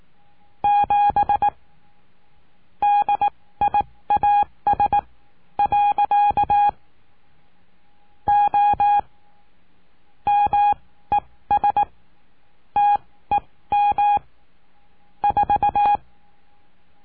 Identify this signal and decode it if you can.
Morse code, heard as a keyed tone, about 18 wpm, spaced as for 14 wpm, 820 Hz. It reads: 7 DIAS. O MES TEM 4
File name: cursos / CURSO DE TELEGRAFIA / Telegrafia - Trilha 20